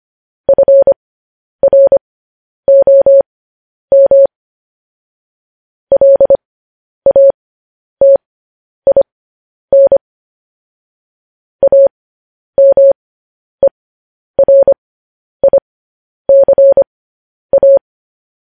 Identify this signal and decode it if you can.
Morse code, heard as a keyed tone, about 25 wpm, spaced as for 10 wpm, 550 Hz. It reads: FROM LATIN AMERICA